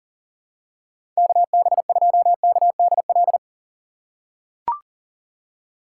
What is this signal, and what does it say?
Morse code, heard as a keyed tone, about 40 wpm, 700 Hz. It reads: KB2KDL
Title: USA Call Signs - 6 Characters - 2x Character Spacing - 40wpm